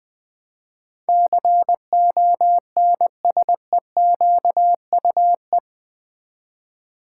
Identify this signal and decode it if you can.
Morse code, heard as a keyed tone, about 20 wpm, 700 Hz. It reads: CONSEQUE
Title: Build A Word - 4 Letters At A Time - 11-Letter Words - 20wpm